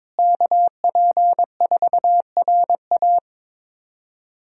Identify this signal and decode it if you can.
Morse code, heard as a keyed tone, about 22 wpm, 700 Hz. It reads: KP4RA
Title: DXCC-100 Call Signs - 5 Characters - Repeated 3x - 22wpm